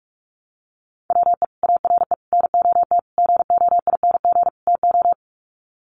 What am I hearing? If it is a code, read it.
Morse code, heard as a keyed tone, about 44 wpm, 700 Hz. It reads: WE ARE NOT GOING TO
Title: Sets of 5 Words 45wpm